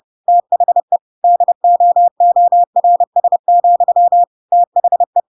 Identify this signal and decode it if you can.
Morse code, heard as a keyed tone, about 30 wpm, 700 Hz. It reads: THE DOORS, THE
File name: Pinocchio / Pinocchio ch6 30 30 ik1hge